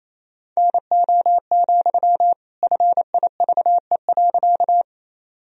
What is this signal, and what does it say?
Morse code, heard as a keyed tone, about 28 wpm, 700 Hz. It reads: NO, FIVE.
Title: Sentences from Top 1000 Words - Review Part 2 - 28wpm